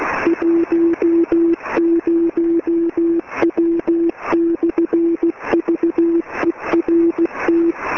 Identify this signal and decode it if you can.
Morse code, heard as a keyed tone, about 16 wpm, 340 Hz. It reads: 10W/VERT
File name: ET3VSC B